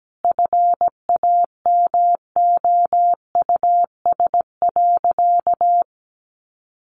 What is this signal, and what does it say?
Morse code, heard as a keyed tone, about 17 wpm, 700 Hz. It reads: FAMOUS.
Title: Sentences from Top 1500 Words 17wpm